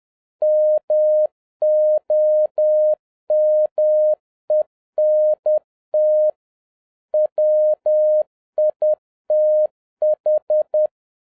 Morse code, heard as a keyed tone, about 10 wpm, 610 Hz. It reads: MOMENT WITH
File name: 2011-07-mp3 / n56 BACK FROM A BIG ACCIDENT 10wpm